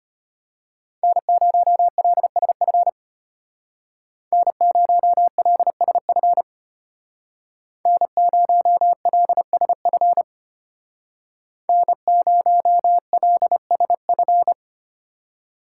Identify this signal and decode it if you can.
Morse code, heard as a keyed tone, about 32 wpm, 700 Hz. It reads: N0LSF N0LSF N0LSF N0LSF